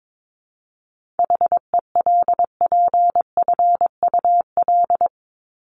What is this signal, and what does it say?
Morse code, heard as a keyed tone, about 22 wpm, 700 Hz. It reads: HELPFUL
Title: Top 751-1000 Words 22wpm